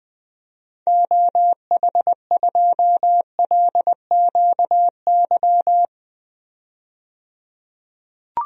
Morse code, heard as a keyed tone, about 21 wpm, 700 Hz. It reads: OH2LQY
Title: DXCC-100 Call Signs - 6 Characters - 20wpm